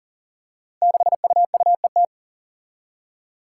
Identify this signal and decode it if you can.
Morse code, heard as a keyed tone, about 40 wpm, 700 Hz. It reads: 6UUET
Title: Five Challenging Letter-Number Combinations 40wpm